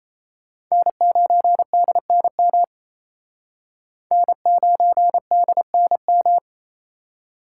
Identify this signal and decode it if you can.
Morse code, heard as a keyed tone, about 35 wpm, 700 Hz. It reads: N9DNM N9DNM
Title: Speed Racing - USA Call Signs 28wpm